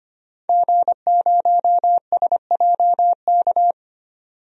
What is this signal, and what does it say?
Morse code, heard as a keyed tone, about 25 wpm, 690 Hz. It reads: G0SJK
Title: DXCC-100 Call Signs - 5 Characters - 25wpm